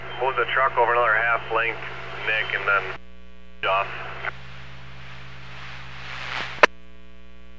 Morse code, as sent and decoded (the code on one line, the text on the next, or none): none